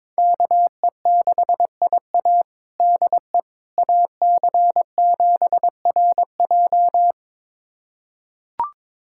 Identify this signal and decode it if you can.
Morse code, heard as a keyed tone, about 22 wpm, 700 Hz. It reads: KE6IA DE AC7RJ
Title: USA Callsign DE Callsign - 5 Characters - 22wpm